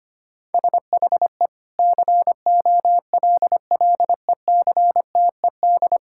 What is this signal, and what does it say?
Morse code, heard as a keyed tone, about 25 wpm, 700 Hz. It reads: SHE COLLECTED
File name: practice / 25wpm / Sentences with 2-Word Phrases - 25wpm